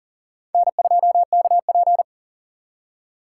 Morse code, heard as a keyed tone, about 52 wpm, 700 Hz. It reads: N2KP